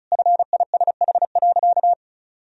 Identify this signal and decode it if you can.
Morse code, heard as a keyed tone, about 34 wpm, 700 Hz. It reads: FISH.
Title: Sentences from Top 1000 Words - Review Part 1 - 35wpm